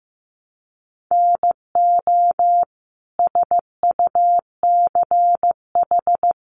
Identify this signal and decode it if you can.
Morse code, heard as a keyed tone, about 15 wpm, 700 Hz. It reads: NO SUCH
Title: Sets of 5 Words Encore 15wpm